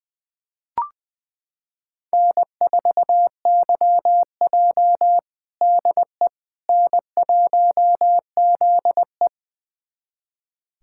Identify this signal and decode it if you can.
Morse code, heard as a keyed tone, about 20 wpm, 700 Hz. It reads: N4YJ DE N1ZE
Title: USA Callsign DE Callsign - 4 Characters - 20wpm